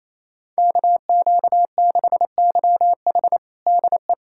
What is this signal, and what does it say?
Morse code, heard as a keyed tone, about 28 wpm, 700 Hz. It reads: KQ6YH DE
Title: USA Callsign DE Callsign 28wpm